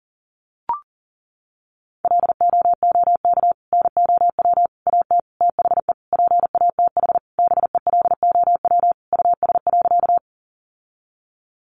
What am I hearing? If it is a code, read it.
Morse code, heard as a keyed tone, about 40 wpm, 700 Hz. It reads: LOOK NOW AT THE PATH BELOW US.
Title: Sentences from Top 1400 Words 40wpm